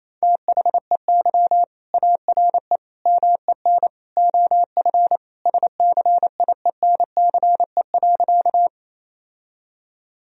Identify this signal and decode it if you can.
Morse code, heard as a keyed tone, about 27 wpm, 700 Hz. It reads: THEY ARE MEN OF SCIENCE.
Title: Sentences from Top 1000 Words - Review Part 2 - 28wpm